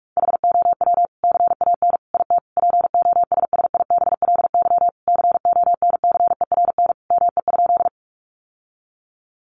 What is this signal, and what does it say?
Morse code, heard as a keyed tone, about 44 wpm, 700 Hz. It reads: HOW CAN IT POSSIBLY CONCERN ME?
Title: Sentences from Top 1300 Words 45wpm